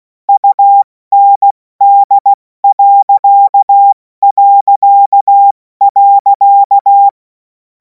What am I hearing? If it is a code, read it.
Morse code, heard as a keyed tone, about 16 wpm, 800 Hz. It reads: UND...
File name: Morsekurs / Witze / Witz 03. März 2026 70 F80